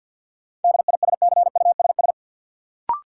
Morse code, heard as a keyed tone, about 52 wpm, 700 Hz. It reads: DISCUSS
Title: Sentences from Top 1000 Words - New Word Drill - 50wpm